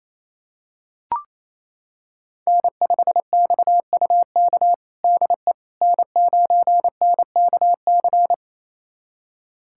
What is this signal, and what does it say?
Morse code, heard as a keyed tone, about 28 wpm, 700 Hz. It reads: N5XUK DE N9NKC
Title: USA Callsign DE Callsign - 5 Characters - 28wpm